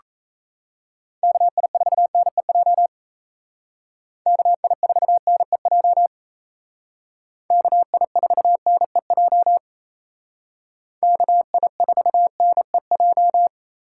Morse code, heard as a keyed tone, about 34 wpm, 700 Hz. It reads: KI4NEJ KI4NEJ KI4NEJ KI4NEJ
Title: Speed Racing - USA Call Signs 28wpm